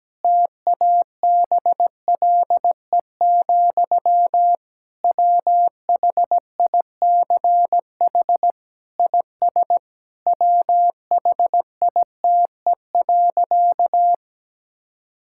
Morse code, heard as a keyed tone, about 17 wpm, 700 Hz. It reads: TABLE, WHICH IS WHITE.
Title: Sentences from Top 1600 Words 17wpm